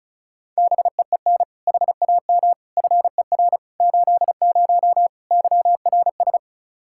Morse code, heard as a keyed tone, about 36 wpm, 700 Hz. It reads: BEEN HAM FER 80 YRS
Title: K1USN SST QSO Practice 35wpm